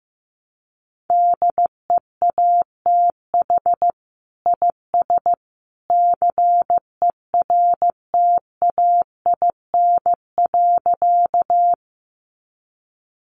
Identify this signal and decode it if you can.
Morse code, heard as a keyed tone, about 15 wpm, 700 Hz. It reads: DEATH IS CERTAIN.
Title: Sentences from Top 1100 Words 15wpm